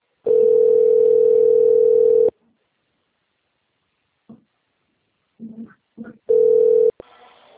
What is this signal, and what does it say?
Ringback tone (North America), 440 + 480 Hz.